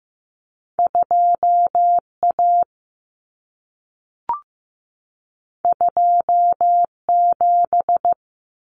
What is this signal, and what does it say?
Morse code, heard as a keyed tone, about 15 wpm, 700 Hz. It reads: 2A 27
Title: Common Cut Numbers - 2-Digit 15wpm